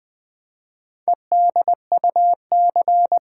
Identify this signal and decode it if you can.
Morse code, heard as a keyed tone, about 20 wpm, 700 Hz. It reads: EDUC
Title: Build A Word - 4 Letters At A Time - 11-Letter Words - 20wpm